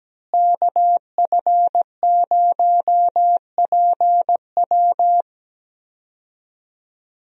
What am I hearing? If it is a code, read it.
Morse code, heard as a keyed tone, about 17 wpm, 700 Hz. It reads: KF0PW
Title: USA Call Signs - 5 Characters - Repeated 3x - 17wpm